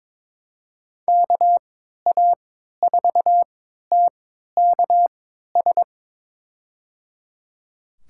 Morse code, heard as a keyed tone, about 22 wpm, spaced as for 13 wpm, 700 Hz. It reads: KA4TKS
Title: USA Call Signs - 3x Character Spacing - 22wpm